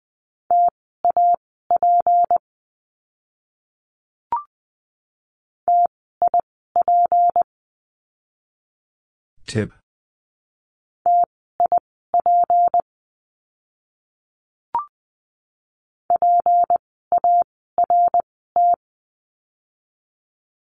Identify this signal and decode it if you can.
Morse code, heard as a keyed tone, about 20 wpm, spaced as for 15 wpm, 700 Hz. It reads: TAP TIP TIP PART
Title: 153 Lesson 25f - Letter p - Words 20wpm